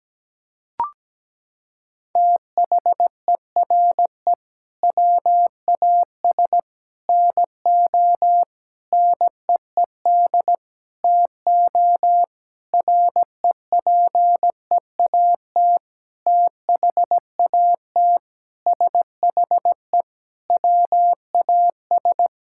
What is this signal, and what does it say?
Morse code, heard as a keyed tone, about 17 wpm, 700 Hz. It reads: THERE WAS NO NEED TO REPEAT THAT SHE WAS
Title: Sentences from Top 1600 Words 17wpm